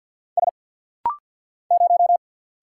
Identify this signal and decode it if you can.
Morse code, heard as a keyed tone, about 50 wpm, 700 Hz. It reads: S 0
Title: Single Letter-Number - Rapid-Fire 50wpm